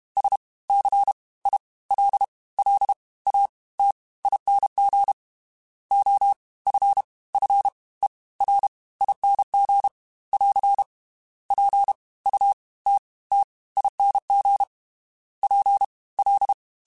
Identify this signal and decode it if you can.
Morse code, heard as a keyed tone, about 32 wpm, spaced as for 20 wpm, 800 Hz. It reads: SCILLATING OFFERING <AR> PUTTING PL
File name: Special INGs / Special INGs-mp3s / Fast ing @30